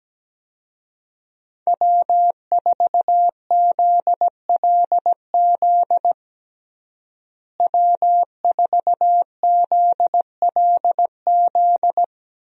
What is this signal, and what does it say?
Morse code, heard as a keyed tone, about 17 wpm, 700 Hz. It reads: W4ZLZ W4ZLZ
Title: USA Call Signs - 5 Characters - Repeated 3x - 17wpm